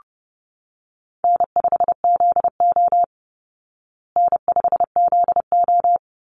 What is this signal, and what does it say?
Morse code, heard as a keyed tone, about 30 wpm, 700 Hz. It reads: N5ZO N5ZO
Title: Call Signs Repeated 3x - 30wpm